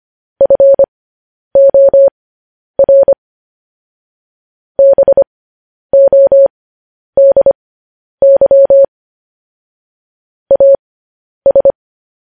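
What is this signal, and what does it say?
Morse code, heard as a keyed tone, about 25 wpm, spaced as for 10 wpm, 550 Hz. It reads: FOR BODY AS
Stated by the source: Morse code transmission of news headlines.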